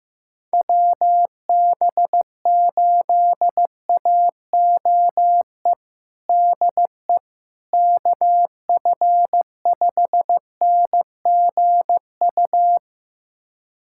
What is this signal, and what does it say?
Morse code, heard as a keyed tone, about 16 wpm, 700 Hz. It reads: WB8AOE DE KF5NGU
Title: USA Callsign DE Callsign - 6 Characters - 15wpm